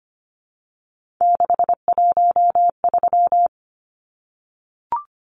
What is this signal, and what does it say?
Morse code, heard as a keyed tone, about 25 wpm, 700 Hz. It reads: 613
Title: Numbers - 3-Digit 25wpm